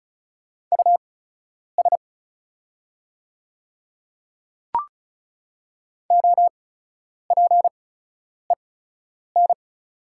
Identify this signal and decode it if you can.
Morse code, heard as a keyed tone, about 34 wpm, spaced as for 10 wpm, 700 Hz. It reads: US OPEN
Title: 8x Character Spacing - Top 300 Words - 35wpm